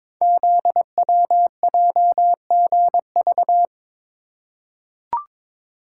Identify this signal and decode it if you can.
Morse code, heard as a keyed tone, about 22 wpm, 700 Hz. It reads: ZWJGV
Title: Code Groups - 5 Letters - 22wpm